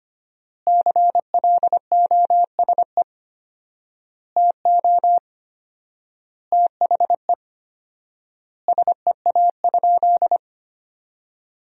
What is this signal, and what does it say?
Morse code, heard as a keyed tone, about 25 wpm, 700 Hz. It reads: CLOSE TO THE SEA?
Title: 4x Word-Spacing - Sentences from Top 300 Words - 25wpm